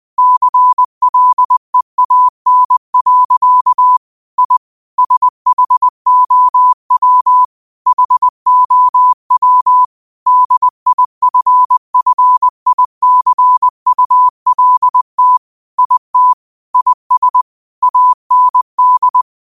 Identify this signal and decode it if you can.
Morse code, heard as a keyed tone, about 20 wpm, 1000 Hz. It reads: CLEAN. I SHOW HOW DIFFICULT IT IS AND